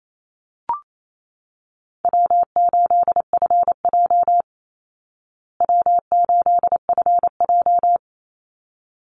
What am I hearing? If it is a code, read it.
Morse code, heard as a keyed tone, about 28 wpm, 700 Hz. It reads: W8FJ W8FJ
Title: Easy Call Signs Repeated 2x - 28wpm